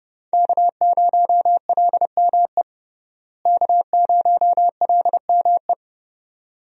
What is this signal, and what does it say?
Morse code, heard as a keyed tone, about 30 wpm, 700 Hz. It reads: K0LME K0LME
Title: USA Call Signs - 5 Characters - Repeated 2x - 30wpm